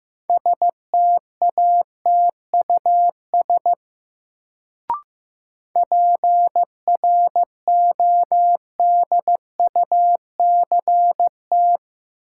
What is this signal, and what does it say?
Morse code, heard as a keyed tone, about 15 wpm, 700 Hz. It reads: STATUS PRODUCT
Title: Sentences from Top 1400 Words - New Word Drill Alt - 15wpm